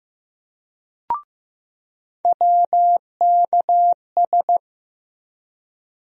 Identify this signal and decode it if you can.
Morse code, heard as a keyed tone, about 15 wpm, 700 Hz. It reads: WKS
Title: USA Call Sign Postfixes - 3 Letters 15wpm